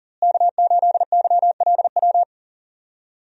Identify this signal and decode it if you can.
Morse code, heard as a keyed tone, about 37 wpm, 700 Hz. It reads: K8YLW
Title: USA Call Signs - 5 Characters - Repeated 3x - 40wpm